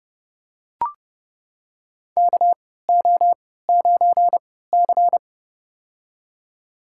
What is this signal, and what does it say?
Morse code, heard as a keyed tone, about 30 wpm, spaced as for 17 wpm, 700 Hz. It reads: KO9C
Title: USA Call Signs - 4 Characters - 3x Character Spacing - 30wpm